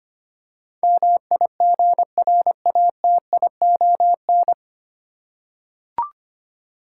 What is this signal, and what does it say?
Morse code, heard as a keyed tone, about 25 wpm, 700 Hz. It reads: MIGRATION
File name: Nine Letter Words - 25wpm